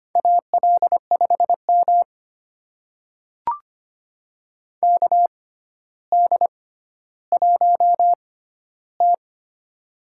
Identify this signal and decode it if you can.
Morse code, heard as a keyed tone, about 25 wpm, spaced as for 13 wpm, 700 Hz. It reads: AL5M KD1T
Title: USA Call Signs - 4 Characters - 6x Character Spacing - 25wpm